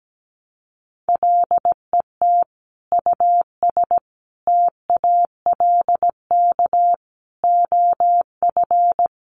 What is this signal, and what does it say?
Morse code, heard as a keyed tone, about 17 wpm, 700 Hz. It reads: LET US TALK OF
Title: Sentences from Top 800 Words 17wpm